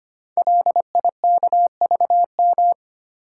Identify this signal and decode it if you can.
Morse code, heard as a keyed tone, about 25 wpm, 700 Hz. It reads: LIKVM
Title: Code Groups - 5 Letters - 25wpm